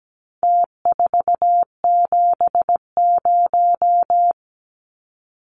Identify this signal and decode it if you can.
Morse code, heard as a keyed tone, about 17 wpm, 700 Hz. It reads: T470
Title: Numbers - Contest Style - 4-Digit 17wpm